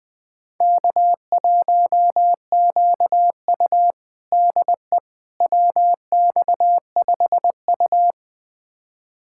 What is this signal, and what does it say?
Morse code, heard as a keyed tone, about 20 wpm, 700 Hz. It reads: K1QU DE WX5U
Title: USA Callsign DE Callsign - 4 Characters - 20wpm